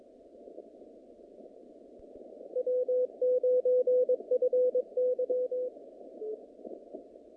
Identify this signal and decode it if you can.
Morse code, heard as a keyed tone, about 24 wpm, 500 Hz. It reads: W9FY
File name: contests / naqp-cw-k5tr-op-k5pi- -2200